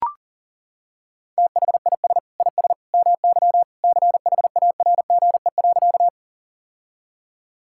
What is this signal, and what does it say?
Morse code, heard as a keyed tone, about 37 wpm, 700 Hz. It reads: THIS IS MY CHARGE.
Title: Sentences from Top 1000 Words - Review Part 6 - 40wpm